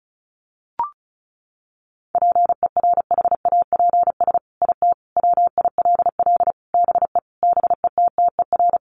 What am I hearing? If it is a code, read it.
Morse code, heard as a keyed tone, about 34 wpm, 700 Hz. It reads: PERHAPS IT WILL BE BETTER
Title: Sentences from Top 1500 Words 35wpm